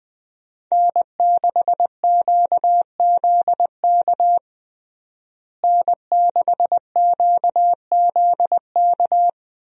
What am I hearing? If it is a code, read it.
Morse code, heard as a keyed tone, about 20 wpm, 700 Hz. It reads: N6QZK N6QZK
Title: USA Call Signs - 5 Characters - Repeated 3x - 20wpm